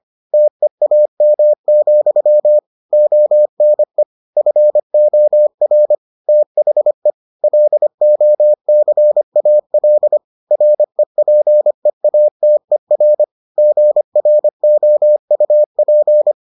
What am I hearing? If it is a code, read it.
Morse code, heard as a keyed tone, about 25 wpm, 600 Hz. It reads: TEAM, ONE FOR THE LOCAL REPEATER GROUP